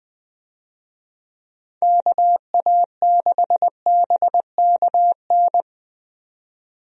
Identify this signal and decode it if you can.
Morse code, heard as a keyed tone, about 20 wpm, 700 Hz. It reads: KA6BKN
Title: USA Call Signs - 6 Characters - 20wpm